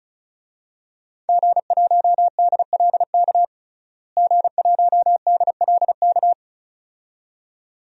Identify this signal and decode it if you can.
Morse code, heard as a keyed tone, about 36 wpm, 700 Hz. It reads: G1DLK G1DLK
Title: DXCC-100 Call Signs - 5 Characters - Repeated 2x - 35wpm